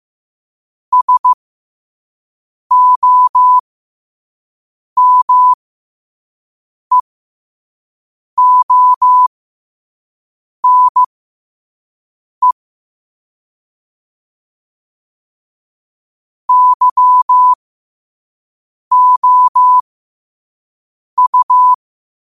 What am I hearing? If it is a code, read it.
Morse code, heard as a keyed tone, about 15 wpm, spaced as for 5 wpm, 1000 Hz. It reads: SOMEONE YOU
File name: qotdfiles / 5wpm